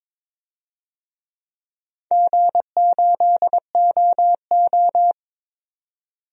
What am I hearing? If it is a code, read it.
Morse code, heard as a keyed tone, about 22 wpm, 700 Hz. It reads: G8OO
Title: DXCC-100 Call Signs - 4 Characters - 22wpm